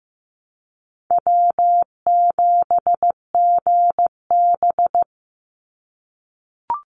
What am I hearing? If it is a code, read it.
Morse code, heard as a keyed tone, about 15 wpm, 700 Hz. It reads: W7GB
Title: Easy Call Signs - 15wpm